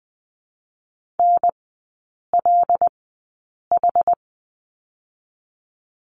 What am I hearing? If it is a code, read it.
Morse code, heard as a keyed tone, about 20 wpm, 700 Hz. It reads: N L H
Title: 145 Lesson 24e - Letter y - Three Characters 20wpm